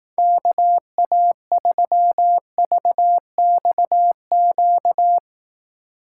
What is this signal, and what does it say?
Morse code, heard as a keyed tone, about 18 wpm, 700 Hz. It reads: KA3VXQ
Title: Speed Racing - USA Call Signs 15wpm